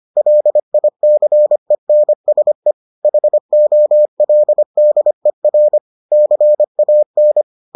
Morse code, heard as a keyed tone, about 25 wpm, 600 Hz. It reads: LICENSE HOLDER CAN